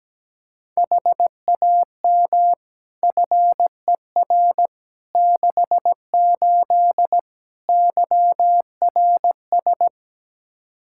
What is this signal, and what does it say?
Morse code, heard as a keyed tone, about 17 wpm, 700 Hz. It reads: HAM FER 68 YRS
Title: K1USN SST QSO Practice 17wpm